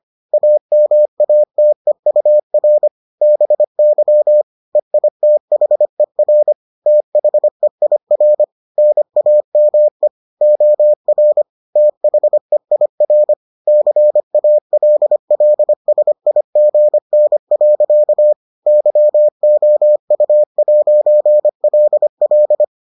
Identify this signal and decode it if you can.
Morse code, heard as a keyed tone, about 25 wpm, 600 Hz. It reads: AMATEUR BY EITHER THEIR NAME OR THEIR CALLSIGN. YOU'LL